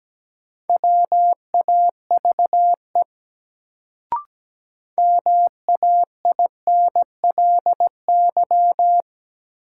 Morse code, heard as a keyed tone, about 17 wpm, 700 Hz. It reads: WAVE MAINLY
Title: Sentences from Top 1400 Words - New Word Drill - 17wpm